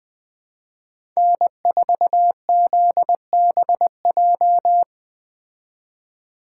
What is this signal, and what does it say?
Morse code, heard as a keyed tone, about 21 wpm, 700 Hz. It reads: N4ZBJ
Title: Speed Racing - USA 1x3 Call Signs 17wpm